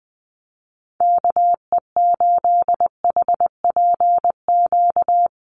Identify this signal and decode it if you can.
Morse code, heard as a keyed tone, about 20 wpm, 700 Hz. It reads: KE8HPQ
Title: USA Call Signs Repeated 3x 20wpm